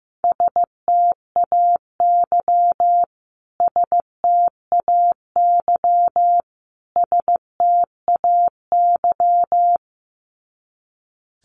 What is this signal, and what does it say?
Morse code, heard as a keyed tone, about 15 wpm, 700 Hz. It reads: STAY STAY STAY
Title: Top 500 Words Repeated 15wpm